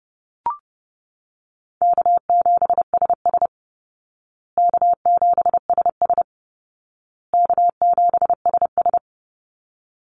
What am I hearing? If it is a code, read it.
Morse code, heard as a keyed tone, about 28 wpm, 700 Hz. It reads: K7SS K7SS K7SS
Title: Call Signs Repeated 2x - 30wpm